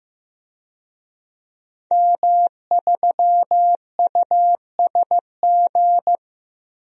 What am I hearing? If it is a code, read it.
Morse code, heard as a keyed tone, about 15 wpm, 700 Hz. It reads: M3USG
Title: DXCC-100 Call Signs - 5 Characters - 15wpm